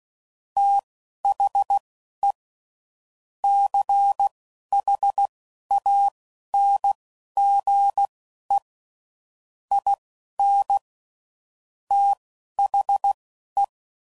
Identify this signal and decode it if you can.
Morse code, heard as a keyed tone, about 16 wpm, spaced as for 11 wpm, 800 Hz. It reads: THE CHANGE IN THE
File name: CW Code Groups @10WPM / Technical Subjects / XC